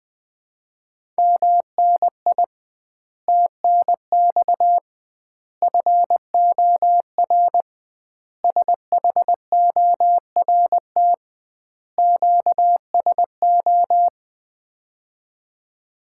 Morse code, heard as a keyed tone, about 20 wpm, 700 Hz. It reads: MNI TNX FOR SHORT QSO
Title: Common QSO Phrases - 2x Word-Spacing - 20wpm